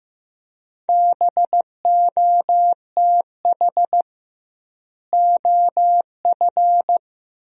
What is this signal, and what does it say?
Morse code, heard as a keyed tone, about 15 wpm, 700 Hz. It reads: BOTH OF
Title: 2x Word-Spacing - Sentences from Top 300 Words - 15wpm